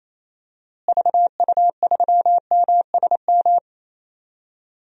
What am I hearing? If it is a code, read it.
Morse code, heard as a keyed tone, about 28 wpm, 700 Hz. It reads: VU3MSM
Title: DXCC-100 Call Signs - 6 Characters - Repeated 3x - 28wpm